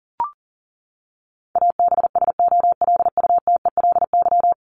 Morse code, heard as a keyed tone, about 37 wpm, 700 Hz. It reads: ABSOLUTELY
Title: LY Suffix - Top 100 Words with Base Words 40wpm